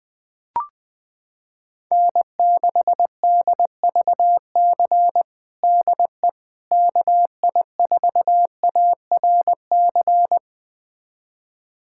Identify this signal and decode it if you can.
Morse code, heard as a keyed tone, about 20 wpm, 700 Hz. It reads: N6DVC DE KI4ARC
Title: USA Callsign DE Callsign 20wpm